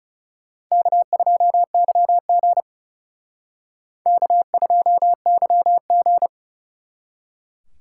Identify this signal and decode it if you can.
Morse code, heard as a keyed tone, about 37 wpm, 700 Hz. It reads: K2YG K2YG
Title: Speed Racing - USA 1x2 Call Signs 30wpm